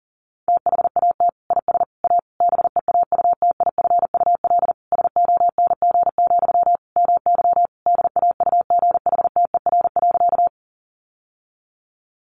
Morse code, heard as a keyed tone, about 40 wpm, 700 Hz. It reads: THAT IS A BEAUTIFUL SONG, MY DAUGHTER.
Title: Sentences from Top 1500 Words 40wpm